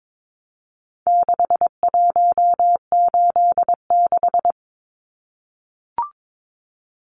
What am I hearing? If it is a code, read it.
Morse code, heard as a keyed tone, about 22 wpm, 700 Hz. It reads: 6186
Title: Numbers - 4-Digit 22wpm